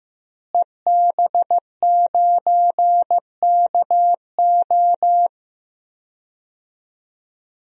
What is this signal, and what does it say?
Morse code, heard as a keyed tone, about 15 wpm, 700 Hz. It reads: EB9KO
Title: DXCC-100 Call Signs - 5 Characters - Repeated 3x - 15wpm